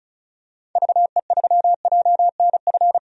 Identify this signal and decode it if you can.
Morse code, heard as a keyed tone, about 36 wpm, 700 Hz. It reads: VE3JNF
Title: DXCC-100 Call Signs - 6 Characters - 35wpm